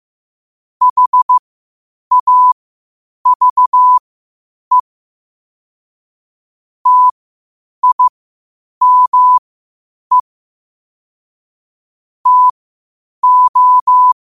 Morse code, heard as a keyed tone, about 15 wpm, spaced as for 8 wpm, 1000 Hz. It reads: HAVE TIME TO
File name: qotdfiles / 8wpm